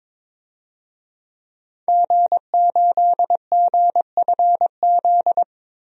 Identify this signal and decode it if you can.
Morse code, heard as a keyed tone, about 21 wpm, 700 Hz. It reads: G8GFZ
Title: DXCC-100 Call Signs - 5 Characters - 22wpm